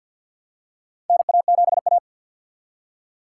Morse code, heard as a keyed tone, about 47 wpm, 700 Hz. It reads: NA7A
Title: USA Call Signs - 4 Characters - 2x Character Spacing - 50wpm